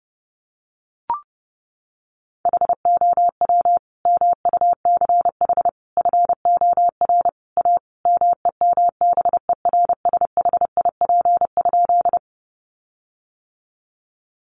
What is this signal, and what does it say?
Morse code, heard as a keyed tone, about 30 wpm, 700 Hz. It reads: HOW MUCH FOR A MEMBERSHIP?
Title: Sentences from Top 1400 Words 30wpm